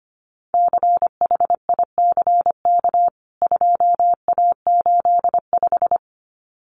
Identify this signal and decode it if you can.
Morse code, heard as a keyed tone, about 25 wpm, 700 Hz. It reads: CHICK 2A85
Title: CWT Exchange 25wpm